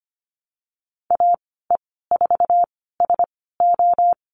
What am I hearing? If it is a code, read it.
Morse code, heard as a keyed tone, about 25 wpm, spaced as for 16 wpm, 700 Hz. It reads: AE4SO
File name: Lesson 15g - Number 5 - Call Signs 25wpm